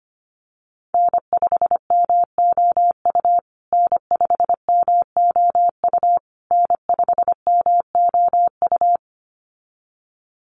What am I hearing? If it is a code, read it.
Morse code, heard as a keyed tone, about 25 wpm, 700 Hz. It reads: N5MOU N5MOU N5MOU
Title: USA Call Signs Repeated 3x 25wpm